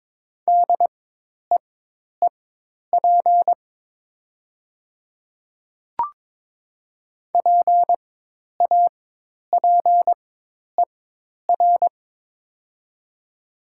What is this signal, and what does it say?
Morse code, heard as a keyed tone, about 23 wpm, spaced as for 10 wpm, 700 Hz. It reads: DEEP PAPER
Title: 4x Character Spacing - Top 500 Words - 22wpm